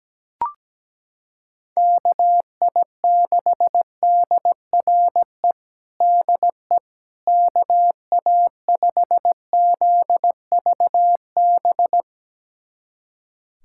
Morse code, heard as a keyed tone, about 17 wpm, 700 Hz. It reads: KI6DRE DE KA5ZVB
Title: USA Callsign DE Callsign 17wpm